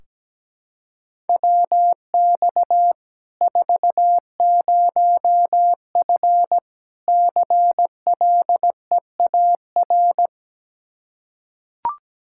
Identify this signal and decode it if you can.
Morse code, heard as a keyed tone, about 17 wpm, 700 Hz. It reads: WX 40F CLEAR
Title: Practice QSO - Exchange 2 - Info - 17wpm